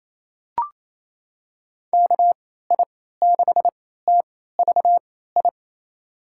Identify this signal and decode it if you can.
Morse code, heard as a keyed tone, about 28 wpm, spaced as for 16 wpm, 700 Hz. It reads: KI6TVI
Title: USA Call Signs - 6 Characters - 3x Character Spacing - 28wpm